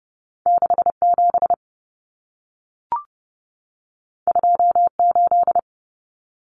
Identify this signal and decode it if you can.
Morse code, heard as a keyed tone, about 30 wpm, 700 Hz. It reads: 67 28
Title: Common Cut Numbers - 2-Digit 30wpm